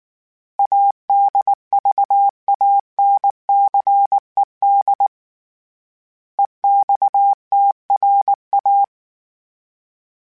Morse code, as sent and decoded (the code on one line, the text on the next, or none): .- -.. ...- .- -. -.-. -.. / . -..- - .-. .-
ADVANCED EXTRA